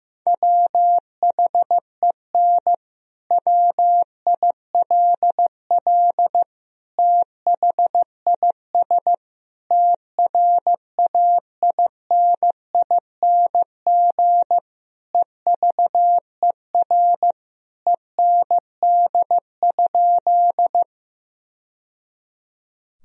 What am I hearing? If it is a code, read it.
Morse code, heard as a keyed tone, about 15 wpm, 700 Hz. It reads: WHEN WILL THIS TRAINING EVER END?
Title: Sentences from Top 1000 Words - Review Part 4 - 15wpm